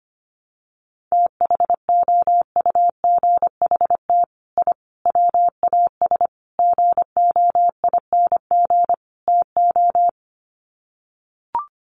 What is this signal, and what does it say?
Morse code, heard as a keyed tone, about 25 wpm, 700 Hz. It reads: THOUGHT I WAS GOING TO
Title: Sets of 5 Words Encore2 25wpm